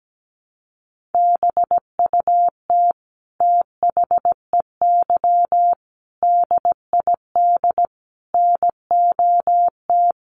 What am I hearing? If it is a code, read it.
Morse code, heard as a keyed tone, about 17 wpm, 700 Hz. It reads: BUT THEY DID NOT
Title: Sentences from Top 1300 Words 17wpm